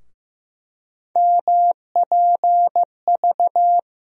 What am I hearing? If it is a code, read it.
Morse code, heard as a keyed tone, about 15 wpm, 700 Hz. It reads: MPV
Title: USA Call Sign Postfixes - 3 Letters 15wpm